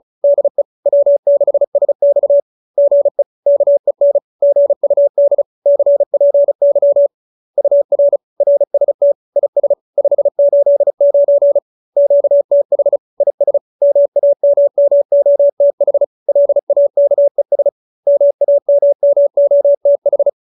Morse code, heard as a keyed tone, about 34 wpm, 570 Hz. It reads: DE W6SX GE KEN GUD CPY UR RST IS 589 QTH IS MAMMOTH LAKES MAMMOTH